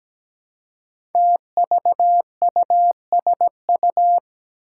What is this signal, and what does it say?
Morse code, heard as a keyed tone, about 17 wpm, 700 Hz. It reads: TVUSU
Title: Five Challenging Letter-Number Combinations 17wpm